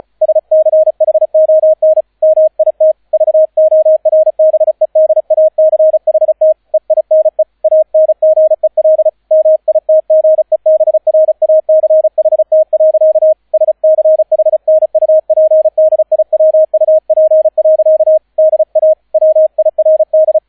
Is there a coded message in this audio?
Yes: Morse code from 0.2 to 20.4 s, heard as a keyed tone, about 34 wpm, 630 Hz, with radio noise. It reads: SCHON MIT VORBEDACHT EINE ANGEL MITGEBRACHT. SCHNUPDIWUP. DA WIRD